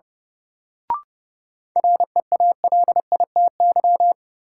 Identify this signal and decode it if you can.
Morse code, heard as a keyed tone, about 29 wpm, 700 Hz. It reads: REALITY
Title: Sentences from Top 1200 Words - New Word Drill Alt - 30wpm